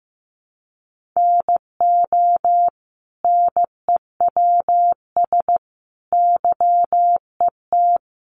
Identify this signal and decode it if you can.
Morse code, heard as a keyed tone, about 15 wpm, 700 Hz. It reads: NO NEWS YET
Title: Sentences from Top 800 Words 15wpm